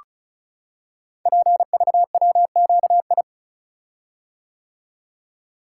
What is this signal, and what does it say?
Morse code, heard as a keyed tone, about 36 wpm, 700 Hz. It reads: PVWQI
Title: Code Groups - 5 Letters - 35wpm